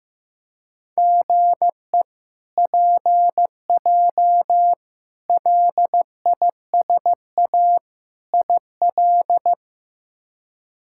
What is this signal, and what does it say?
Morse code, heard as a keyed tone, about 15 wpm, 700 Hz. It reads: GE PJ LISA IL
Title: K1USN SST QSO Practice - 15wpm